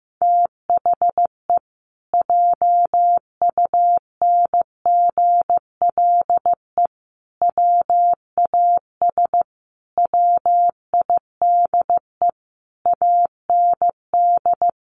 Morse code, heard as a keyed tone, about 15 wpm, 700 Hz. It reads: THE JUNGLE WAS WIDE AND
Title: A Sound Of Thunder 15wpm